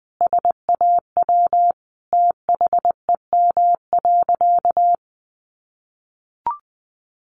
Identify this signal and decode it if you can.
Morse code, heard as a keyed tone, about 20 wpm, 700 Hz. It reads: SAW THEM.
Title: Sentences from Top 1300 Words 20wpm